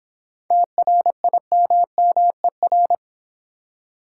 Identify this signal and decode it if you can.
Morse code, heard as a keyed tone, about 22 wpm, 700 Hz. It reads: TRIMMER
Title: Difficult Words - Speed-Racing with Hints - 20wpm